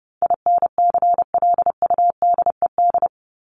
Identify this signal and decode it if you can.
Morse code, heard as a keyed tone, about 30 wpm, 700 Hz. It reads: INCLUDED
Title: ED Suffix - Top 100 Words with Base Words 30wpm